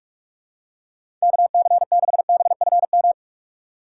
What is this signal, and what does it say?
Morse code, heard as a keyed tone, about 46 wpm, 700 Hz. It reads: KC6BFM
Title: USA Call Signs - 6 Characters - 2x Character Spacing - 45wpm